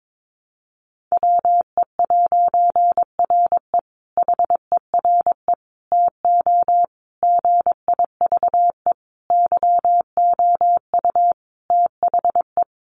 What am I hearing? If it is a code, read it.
Morse code, heard as a keyed tone, about 22 wpm, 700 Hz. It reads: WE'RE HERE TO GIVE YOU THE
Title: A Sound Of Thunder 22wpm